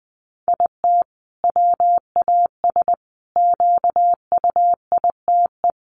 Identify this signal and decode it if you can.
Morse code, heard as a keyed tone, about 20 wpm, 700 Hz. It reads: IT WAS QUITE
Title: Sentences from Top 700 Words 20wpm